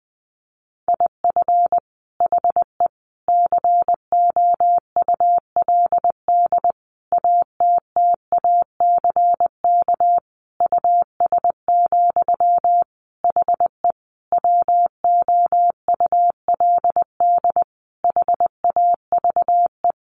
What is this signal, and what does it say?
Morse code, heard as a keyed tone, about 20 wpm, 700 Hz. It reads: IF HE COULD ATTACK US, HE WOULD HAVE
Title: Sentences from Top 1500 Words 20wpm